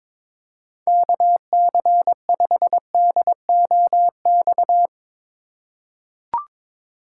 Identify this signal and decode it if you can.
Morse code, heard as a keyed tone, about 21 wpm, 700 Hz. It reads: KC5DOX
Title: USA Call Signs - 6 Characters - 4x Character Spacing - 22wpm